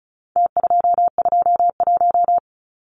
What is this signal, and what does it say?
Morse code, heard as a keyed tone, about 34 wpm, 700 Hz. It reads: T221
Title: Numbers - Contest Style - 4-Digit 35wpm